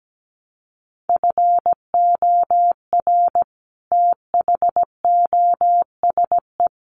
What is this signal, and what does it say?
Morse code, heard as a keyed tone, about 17 wpm, 700 Hz. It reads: FOR THOSE
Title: Sets of 5 Words Encore 17wpm